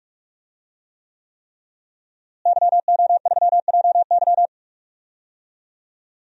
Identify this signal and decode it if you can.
Morse code, heard as a keyed tone, about 46 wpm, 700 Hz. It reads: YO3JY
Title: DXCC-100 Call Signs - 5 Characters - Repeated 3x - 45wpm